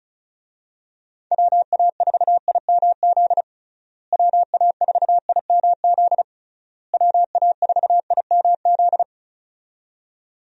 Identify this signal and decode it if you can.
Morse code, heard as a keyed tone, about 34 wpm, 700 Hz. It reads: WA4IMZ WA4IMZ WA4IMZ
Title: USA Call Signs - 6 Characters - Repeated 3x - 35wpm